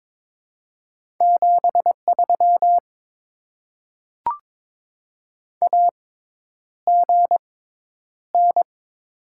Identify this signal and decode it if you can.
Morse code, heard as a keyed tone, about 22 wpm, spaced as for 10 wpm, 700 Hz. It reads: 73 AGN
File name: Top 20 QSO Elements - 6x Character Spacing - 22wpm